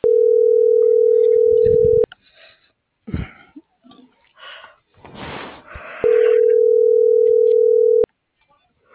Ringback tone (North America), 440 + 480 Hz.
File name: CENTRAL BLN / out ximena.diaz - .73866